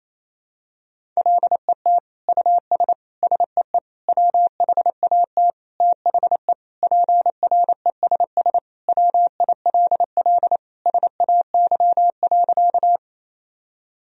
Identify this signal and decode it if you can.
Morse code, heard as a keyed tone, about 28 wpm, 700 Hz. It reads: LET US SEE WHAT THE PRESS WILL SAY.
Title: Sentences from Top 1000 Words - Review Part 1 - 28wpm